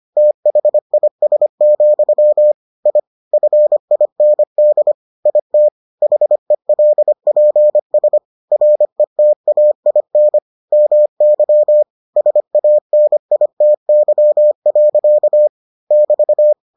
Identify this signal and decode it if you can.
Morse code, heard as a keyed tone, about 25 wpm, 600 Hz. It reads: THIS, I FIND IT HELPS RETAIN MY SANITY. <BT>